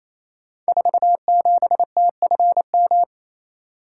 Morse code, heard as a keyed tone, about 28 wpm, 700 Hz. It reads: 47TFM
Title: Code Groups - 5 Letter:Numbers - 28wpm